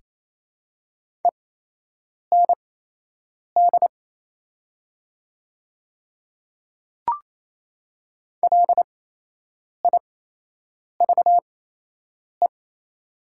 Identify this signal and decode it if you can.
Morse code, heard as a keyed tone, about 28 wpm, spaced as for 8 wpm, 700 Hz. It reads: END LIVE
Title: 8x Character Spacing - Top 200 Words - 28wpm